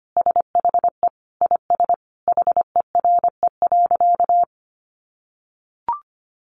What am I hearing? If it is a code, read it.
Morse code, heard as a keyed tone, about 25 wpm, 700 Hz. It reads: SHE IS HERE.
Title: Sentences from Top 600 Words 25wpm